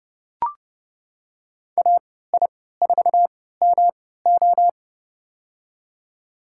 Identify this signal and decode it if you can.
Morse code, heard as a keyed tone, about 31 wpm, spaced as for 17 wpm, 700 Hz. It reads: AI4MO
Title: USA Call Signs - 3x Character Spacing - 30wpm